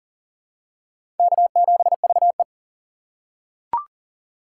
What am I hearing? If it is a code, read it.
Morse code, heard as a keyed tone, about 40 wpm, 700 Hz. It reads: K7VE
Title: USA Call Signs - 4x Character Spacing - 40wpm